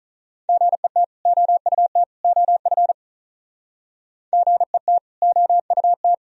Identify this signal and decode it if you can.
Morse code, heard as a keyed tone, about 45 wpm, 700 Hz. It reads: GET OUT OF GET OUT